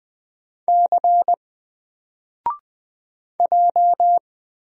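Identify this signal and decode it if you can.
Morse code, heard as a keyed tone, about 20 wpm, 700 Hz. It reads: C J
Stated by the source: Single letter and number Morse drill at 20 wpm, from N4LCD.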